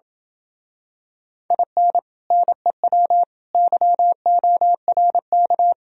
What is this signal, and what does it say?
Morse code, heard as a keyed tone, about 28 wpm, 700 Hz. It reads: IN NEW YORK